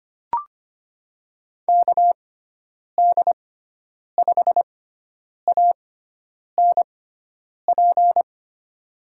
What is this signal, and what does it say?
Morse code, heard as a keyed tone, about 25 wpm, spaced as for 9 wpm, 700 Hz. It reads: KD5ANP
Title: USA Call Signs - 6 Characters - 6x Character Spacing - 25wpm